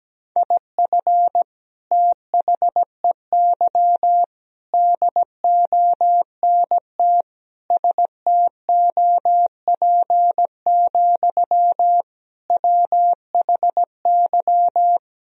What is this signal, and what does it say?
Morse code, heard as a keyed tone, about 17 wpm, 700 Hz. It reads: IF THEY DONT STOP, WHY
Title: Sentences from Top 1000 Words - Review Part 1 - 17wpm